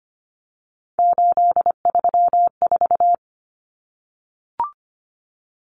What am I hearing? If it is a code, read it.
Morse code, heard as a keyed tone, about 25 wpm, 700 Hz. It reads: 834
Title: Numbers - 3-Digit 25wpm